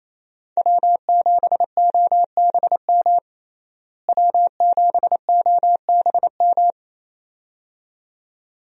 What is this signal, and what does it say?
Morse code, heard as a keyed tone, about 28 wpm, 700 Hz. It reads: W7OBM W7OBM
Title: USA Call Signs - 5 Characters - Repeated 2x - 28wpm